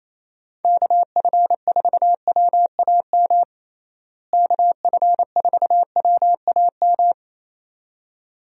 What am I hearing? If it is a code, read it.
Morse code, heard as a keyed tone, about 28 wpm, 700 Hz. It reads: KF4WAM KF4WAM
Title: USA Call Signs - 6 Characters - Repeated 2x - 28wpm